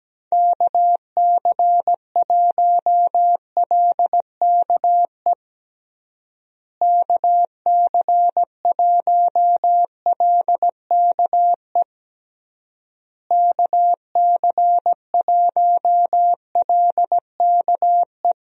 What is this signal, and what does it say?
Morse code, heard as a keyed tone, about 17 wpm, 700 Hz. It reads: KC1LKE KC1LKE KC1LKE
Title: USA Call Signs - 6 Characters - Repeated 3x - 17wpm